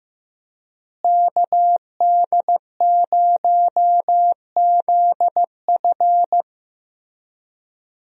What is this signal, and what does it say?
Morse code, heard as a keyed tone, about 15 wpm, 700 Hz. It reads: KD0ZF
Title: USA Call Signs - 5 Characters - Repeated 3x - 15wpm